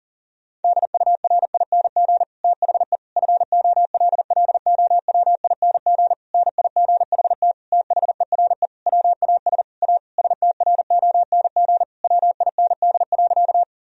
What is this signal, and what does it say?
Morse code, heard as a keyed tone, about 40 wpm, 700 Hz. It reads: DURING THE FOLLOWING NIGHT THERE WAS A STRONG WIND.
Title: Sentences from Top 1000 Words - Review Part 4 - 40wpm